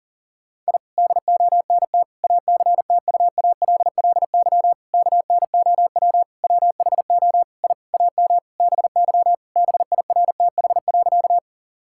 Morse code, heard as a keyed tone, about 40 wpm, 700 Hz. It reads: I DONT ACTUALLY KNOW WHO I AM BY BIRTH.
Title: Sentences from Top 1000 Words - Review Part 5 - 40wpm